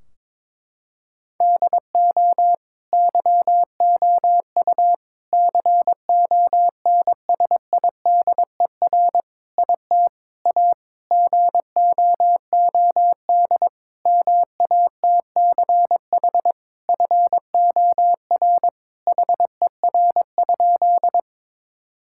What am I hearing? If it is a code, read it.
Morse code, heard as a keyed tone, about 21 wpm, 700 Hz. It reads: DO YOU CONSIDER IT A GOOD MATCH FOR HER?
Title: Sentences from Top 1600 Words 22wpm